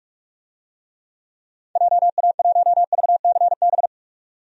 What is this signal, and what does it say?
Morse code, heard as a keyed tone, about 47 wpm, 700 Hz. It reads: JA1VCB
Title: DXCC-100 Call Signs - 6 Characters - 45wpm